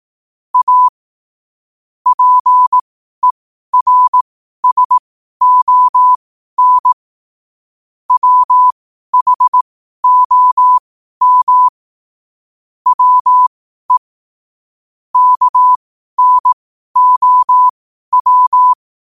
Morse code, heard as a keyed tone, about 18 wpm, spaced as for 12 wpm, 1000 Hz. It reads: A PERSON WHOM WE KNOW
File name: 12wpm